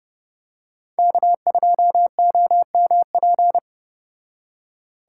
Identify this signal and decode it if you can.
Morse code, heard as a keyed tone, about 29 wpm, 700 Hz. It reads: K2OMP